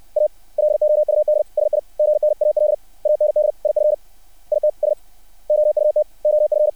Morse code, heard as a keyed tone, about 60 wpm, 600 Hz. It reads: A LOUD SIGNAL AND IF ITS OUT OF
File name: archived / CRSnet